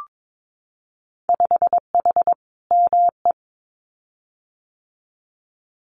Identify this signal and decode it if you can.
Morse code, heard as a keyed tone, about 22 wpm, 700 Hz. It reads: 5H ME
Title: Winter Field Day Exchanges Encore 22wpm